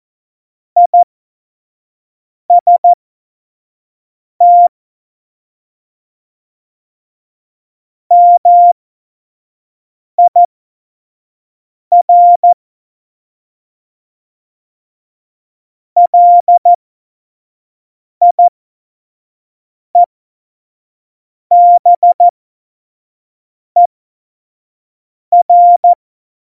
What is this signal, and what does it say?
Morse code, heard as a keyed tone, about 14 wpm, spaced as for 5 wpm, 700 Hz. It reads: IST MIR LIEBER
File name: Morsekurs / Witze / Witz 18. Dezember 2018, 25 F70